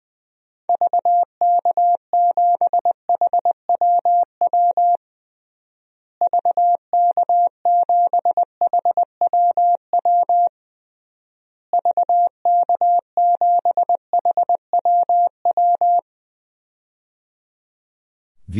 Morse code, heard as a keyed tone, about 20 wpm, 700 Hz. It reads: VK7HWW VK7HWW VK7HWW
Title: DXCC-100 Call Signs - 6 Characters - Repeated 3x - 20wpm